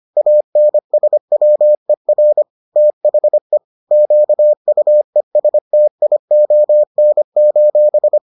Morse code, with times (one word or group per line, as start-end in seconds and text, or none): 0.2-2.4 ANSWER
2.8-3.6 THE
3.9-8.2 QUESTION: